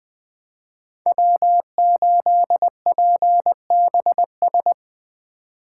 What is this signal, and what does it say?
Morse code, heard as a keyed tone, about 20 wpm, 700 Hz. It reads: W8PBS
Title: Speed Racing - USA 1x3 Call Signs 17wpm